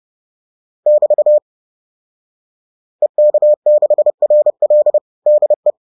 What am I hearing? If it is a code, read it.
Morse code, heard as a keyed tone, about 30 wpm, 600 Hz. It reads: <BT> EK6RL DE
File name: 30wpm-7